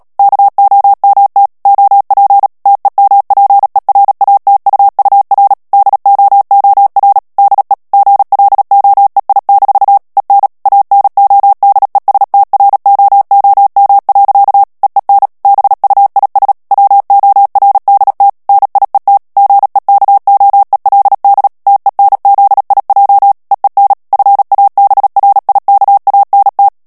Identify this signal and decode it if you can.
Morse code, heard as a keyed tone, about 37 wpm, 800 Hz. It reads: KOMT OP TEMPERATUUR DOOR DE GLOEI- EN ANODESTROOM. EEN BUIS WORDT NIET GEKOELD TENZIJ EEN FABRIKANT
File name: morse mp3 / HOLLANDAIS texte clair 2 vitesse 35 mots minute